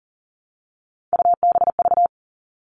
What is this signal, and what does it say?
Morse code, heard as a keyed tone, about 40 wpm, 700 Hz. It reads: UBV
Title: All Cut Numbers - 3-Digit 40wpm